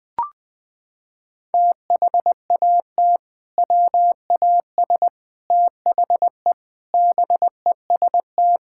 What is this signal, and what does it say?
Morse code, heard as a keyed tone, about 21 wpm, 700 Hz. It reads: THAT WAS THE BEST
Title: Sentences with 3-Word Phrases - 20wpm